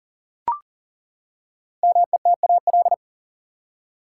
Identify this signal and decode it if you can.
Morse code, heard as a keyed tone, about 40 wpm, 700 Hz. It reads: METAL
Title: Five Letter Words - 40wpm